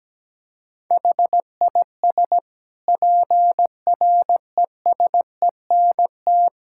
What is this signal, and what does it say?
Morse code, heard as a keyed tone, about 17 wpm, 700 Hz. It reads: HIS PRESENT
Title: Sentences from Top 1000 Words - Review Part 6 - 17wpm